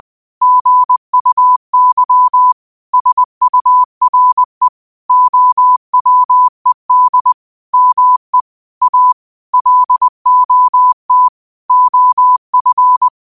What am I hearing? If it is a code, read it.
Morse code, heard as a keyed tone, about 20 wpm, 1000 Hz. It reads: GUY SURE OWED ME A LOT OF